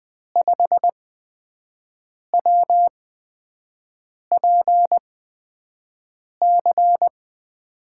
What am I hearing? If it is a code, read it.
Morse code, heard as a keyed tone, about 20 wpm, spaced as for 5 wpm, 700 Hz. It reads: 5WPC
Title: USA Call Signs - 6 Characters - 8x Character Spacing - 20wpm